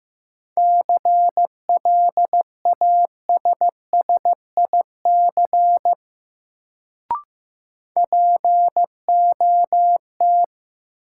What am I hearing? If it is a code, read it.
Morse code, heard as a keyed tone, about 15 wpm, 700 Hz. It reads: CLASSIC POT
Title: Sentences from Top 1000 Words - New Word Drill - 15wpm